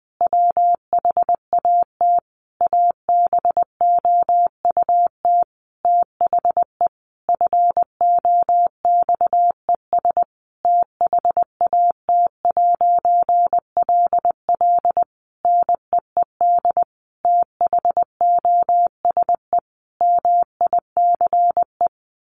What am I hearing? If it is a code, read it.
Morse code, heard as a keyed tone, about 20 wpm, 700 Hz. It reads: WHAT ABOUT THE FOXES THAT'LL NEED THOSE MICE
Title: A Sound Of Thunder 20wpm